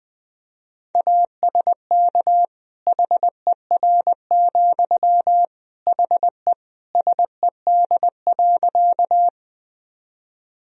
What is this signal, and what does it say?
Morse code, heard as a keyed tone, about 20 wpm, 700 Hz. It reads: ASK HER, HE SED.
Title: Sentences from Top 200 Words and Abbreviations 20wpm